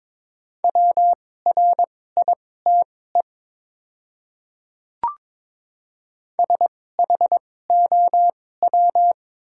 Morse code, heard as a keyed tone, about 22 wpm, spaced as for 16 wpm, 700 Hz. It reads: WRITE SHOW
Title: 2x Character Spacing - Top 200 Words - 22wpm